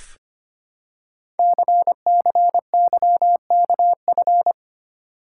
Morse code, heard as a keyed tone, about 25 wpm, 700 Hz. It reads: CCYKF
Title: Code Groups - 5 Letters - 25wpm